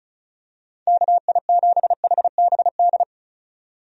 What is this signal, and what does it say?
Morse code, heard as a keyed tone, about 34 wpm, 700 Hz. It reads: KI7HBD
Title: USA Call Signs - 6 Characters - 35wpm